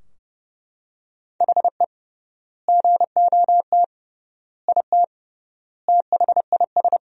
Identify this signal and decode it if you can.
Morse code, heard as a keyed tone, about 30 wpm, 700 Hz. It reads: HE GOT IT THIS
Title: 3x Word-Spacing - Sentences from Top 400 Words - 30wpm